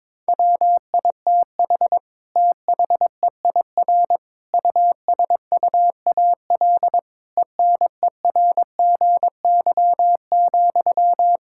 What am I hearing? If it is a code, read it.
Morse code, heard as a keyed tone, about 22 wpm, 700 Hz. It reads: WITH THEIR USUAL ENERGY,
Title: Sentences from Top 1000 Words - Review Part 6 - 22wpm